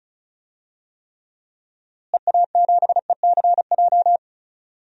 Morse code, heard as a keyed tone, about 34 wpm, 700 Hz. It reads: EA7ECJ
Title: DXCC-100 Call Signs - 6 Characters - Repeated 3x - 35wpm